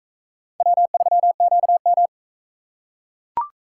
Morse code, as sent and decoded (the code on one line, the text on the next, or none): .-- ...-- --.- --
W3QM